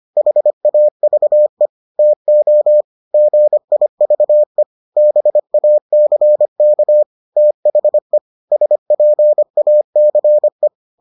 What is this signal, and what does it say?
Morse code, heard as a keyed tone, about 25 wpm, 600 Hz. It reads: HAVE TO GIVE BACK THE SPACE